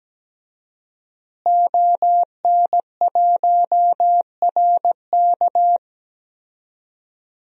Morse code, heard as a keyed tone, about 17 wpm, 700 Hz. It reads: ON1RK
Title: DXCC-100 Call Signs - 5 Characters - Repeated 3x - 17wpm